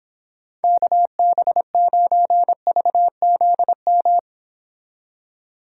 Morse code, heard as a keyed tone, about 26 wpm, 700 Hz. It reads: KB9VZM
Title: Speed Racing - USA 2x3 Call Signs 22wpm